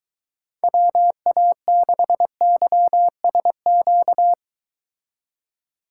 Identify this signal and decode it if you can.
Morse code, heard as a keyed tone, about 24 wpm, 700 Hz. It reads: WA6YSQ